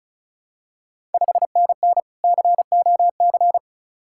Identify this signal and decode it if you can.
Morse code, heard as a keyed tone, about 36 wpm, 690 Hz. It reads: 5NN COC
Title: Ontario QSO Party Exchange 35wpm